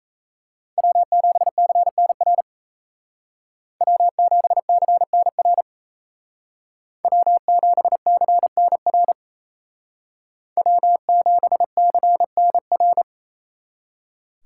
Morse code, heard as a keyed tone, about 33 wpm, 700 Hz. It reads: W7CNR W7CNR W7CNR W7CNR
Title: Speed Racing - USA 1x3 Call Signs 28wpm